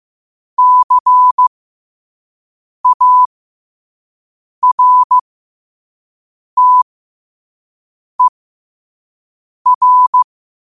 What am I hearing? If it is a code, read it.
Morse code, heard as a keyed tone, about 15 wpm, spaced as for 5 wpm, 1000 Hz. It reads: CARTER
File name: qotdfiles / 5wpm